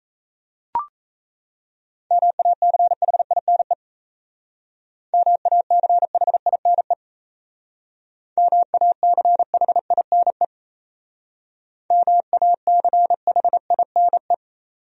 Morse code, heard as a keyed tone, about 36 wpm, 700 Hz. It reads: MACHINE MACHINE MACHINE MACHINE